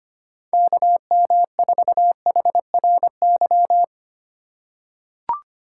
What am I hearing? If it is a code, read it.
Morse code, heard as a keyed tone, about 25 wpm, 700 Hz. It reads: KM4HRY
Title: USA Call Signs - 4x Character Spacing - 25wpm